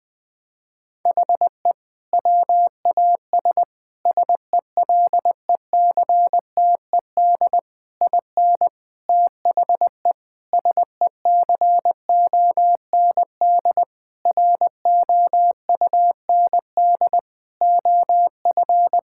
Morse code, heard as a keyed tone, about 19 wpm, 700 Hz. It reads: HE WAS SELECTED IN THE SECOND ROUND OF
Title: Sentences with 6-Word Phrases - Spoken Beginning - 20wpm